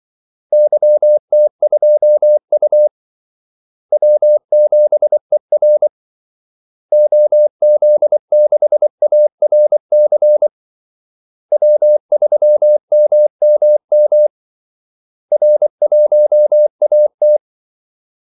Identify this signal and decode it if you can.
Morse code, heard as a keyed tone, about 24 wpm, 600 Hz. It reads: YT2U W7ER OZ6ARC W3MMM R1AT
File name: Callsigns-24wpm